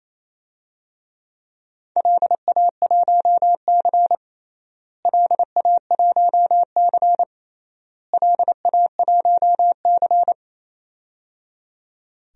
Morse code, heard as a keyed tone, about 28 wpm, 700 Hz. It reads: LA1C LA1C LA1C
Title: DXCC-100 Call Signs - 4 Characters - Repeated 3x - 28wpm